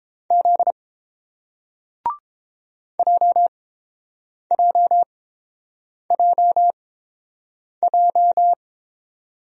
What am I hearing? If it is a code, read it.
Morse code, heard as a keyed tone, about 30 wpm, spaced as for 7 wpm, 700 Hz. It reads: Z JJJJ